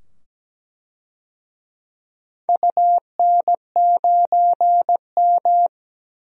Morse code, heard as a keyed tone, about 17 wpm, 700 Hz. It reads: UN9M
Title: DXCC-100 Call Signs - 4 Characters - 17wpm